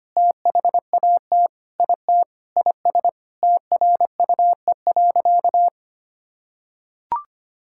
Morse code, heard as a keyed tone, about 25 wpm, 700 Hz. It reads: THAT IT IS TRUE.
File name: practice / 25wpm / Sentences with 4-Word Phrases - 25wpm